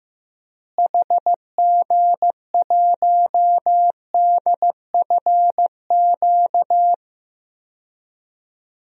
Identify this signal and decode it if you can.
Morse code, heard as a keyed tone, about 15 wpm, 700 Hz. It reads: HG1DFQ
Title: DXCC-100 Call Signs - 6 Characters - Repeated 3x - 15wpm